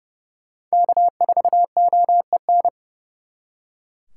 Morse code, heard as a keyed tone, about 37 wpm, 700 Hz. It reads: K4OEN
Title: Speed Racing - USA 1x3 Call Signs 30wpm